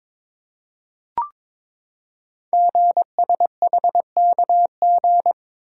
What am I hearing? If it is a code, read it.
Morse code, heard as a keyed tone, about 22 wpm, 690 Hz. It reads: GSHKG
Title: Code Groups - 5 Letters - 22wpm